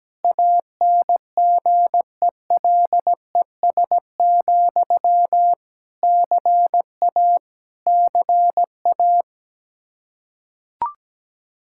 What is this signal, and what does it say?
Morse code, heard as a keyed tone, about 17 wpm, 700 Hz. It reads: ANGELES, CA CA
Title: K1USN SST QSO Practice 17wpm